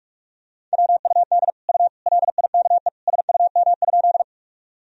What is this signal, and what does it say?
Morse code, heard as a keyed tone, about 46 wpm, 700 Hz. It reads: WUD U LIKE SUM?
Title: Sentences from Top 100 Words and Abbreviations 45wpm